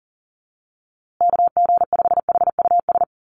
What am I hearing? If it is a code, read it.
Morse code, heard as a keyed tone, about 40 wpm, 700 Hz. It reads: KG5HUS
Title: USA Call Signs Repeated 3x 40wpm